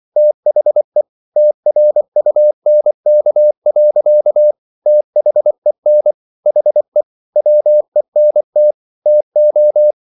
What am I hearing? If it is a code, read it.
Morse code, heard as a keyed tone, about 24 wpm, 600 Hz. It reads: THE TRUNK. THEN HE WENT TO